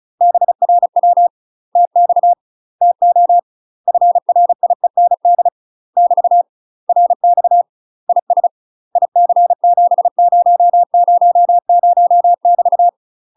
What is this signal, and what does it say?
Morse code, heard as a keyed tone, about 35 wpm, 700 Hz. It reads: BRW TX TO FRIEND <BT> RX IS IC7000 <BT>
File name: UK MASTER QSOs TRAINING / 35WPM / UK MASTER QSOs 0015